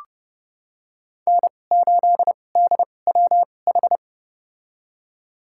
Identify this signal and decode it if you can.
Morse code, heard as a keyed tone, about 30 wpm, spaced as for 22 wpm, 700 Hz. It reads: N8DWH
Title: USA Call Signs - 5 Characters - 2x Character Spacing - 30wpm